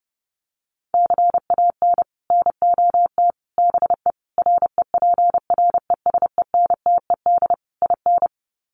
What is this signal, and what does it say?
Morse code, heard as a keyed tone, about 28 wpm, 700 Hz. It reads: CAN NOT BE REPRESENTED IN
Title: Sets of 5 Words 30wpm